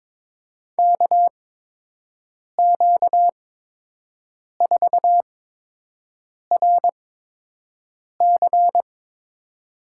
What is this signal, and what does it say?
Morse code, heard as a keyed tone, about 22 wpm, spaced as for 6 wpm, 700 Hz. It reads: KQ4RC
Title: USA Call Signs - 5 Characters - 8x Character Spacing - 22wpm